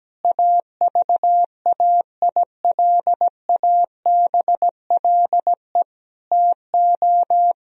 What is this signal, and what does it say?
Morse code, heard as a keyed tone, about 17 wpm, 700 Hz. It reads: AVAILABLE TO
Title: Sentences from Top 1000 Words - Review Part 3 - 17wpm